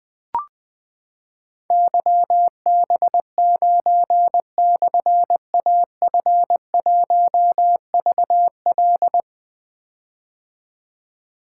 Morse code, heard as a keyed tone, about 20 wpm, 700 Hz. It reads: YB9/AF1VL
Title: Difficult Call Signs 20wpm